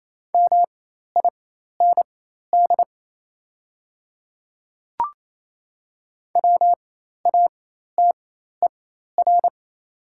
Morse code, heard as a keyed tone, about 28 wpm, spaced as for 13 wpm, 700 Hz. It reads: MIND WATER
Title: 4x Character Spacing - Top 500 Words - 28wpm